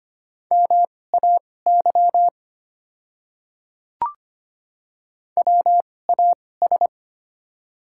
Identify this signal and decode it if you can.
Morse code, heard as a keyed tone, about 25 wpm, spaced as for 18 wpm, 700 Hz. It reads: MAY WAS
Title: 2x Character Spacing - Top 100 Words - 25wpm